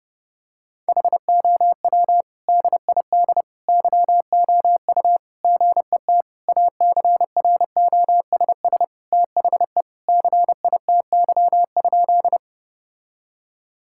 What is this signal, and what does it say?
Morse code, heard as a keyed tone, about 32 wpm, 700 Hz. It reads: HOW DID YOU GET ACROSS THE CITY?
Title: Sentences from Top 1000 Words - Review Part 5 - 30wpm